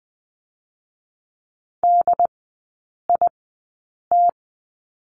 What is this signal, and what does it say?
Morse code, heard as a keyed tone, about 20 wpm, spaced as for 8 wpm, 690 Hz. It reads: DIT
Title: 081 Lesson 14e - Number 2 - Three Characters 20wpm